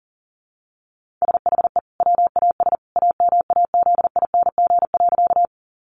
Morse code, heard as a keyed tone, about 40 wpm, 700 Hz. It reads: SHE WAS AMAZING.
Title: Sentences from Top 1500 Words 40wpm